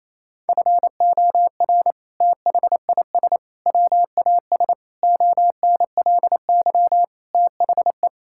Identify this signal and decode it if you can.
Morse code, heard as a keyed tone, about 28 wpm, 700 Hz. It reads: FOR THIS WAS ONLY THE
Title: Sentences from Top 1000 Words - Review Part 5 - 28wpm